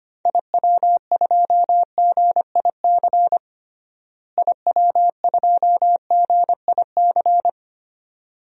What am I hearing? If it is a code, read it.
Morse code, heard as a keyed tone, about 25 wpm, 700 Hz. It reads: IW2GIC IW2GIC
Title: DXCC-100 Call Signs - 6 Characters - Repeated 2x - 25wpm